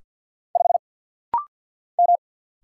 Morse code, heard as a keyed tone, about 50 wpm, 700 Hz. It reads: 5 M
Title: Single Letter-Number Alt - Rapid-Fire 50wpm